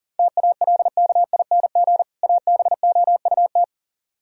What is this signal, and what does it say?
Morse code, heard as a keyed tone, about 50 wpm, 700 Hz. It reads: TALKING ABOUT